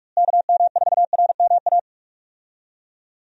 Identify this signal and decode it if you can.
Morse code, heard as a keyed tone, about 46 wpm, 700 Hz. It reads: KM4RMA
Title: USA Call Signs - 6 Characters - Repeated 2x - 45wpm